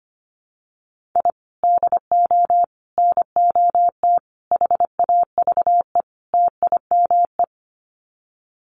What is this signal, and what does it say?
Morse code, heard as a keyed tone, about 25 wpm, 700 Hz. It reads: I DO NOT HAVE TIME
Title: Sets of 5 Words Encore 25wpm